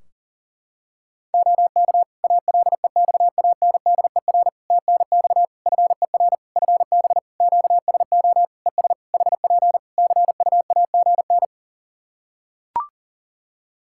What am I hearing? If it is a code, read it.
Morse code, heard as a keyed tone, about 40 wpm, 700 Hz. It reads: OK ALEXANDER TNX FER FB QSO ES HP CUAGN
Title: K1USN SST QSO Practice 40wpm